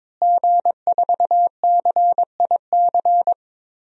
Morse code, heard as a keyed tone, about 22 wpm, 700 Hz. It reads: G4CIC
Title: DXCC-100 Call Signs - 5 Characters - Repeated 3x - 22wpm